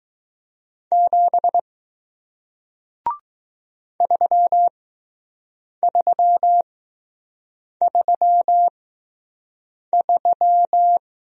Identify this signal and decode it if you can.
Morse code, heard as a keyed tone, about 20 wpm, spaced as for 6 wpm, 700 Hz. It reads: 7 3333